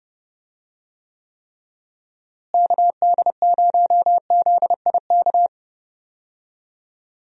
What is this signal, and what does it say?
Morse code, heard as a keyed tone, about 30 wpm, 700 Hz. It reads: KD0ZIK
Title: USA Call Signs - 6 Characters - Repeated 3x - 30wpm